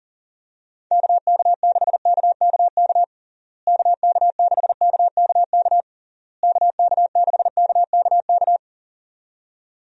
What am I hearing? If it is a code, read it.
Morse code, heard as a keyed tone, about 40 wpm, 700 Hz. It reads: KK6KKK KK6KKK KK6KKK
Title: USA Call Signs - 6 Characters - Repeated 3x - 40wpm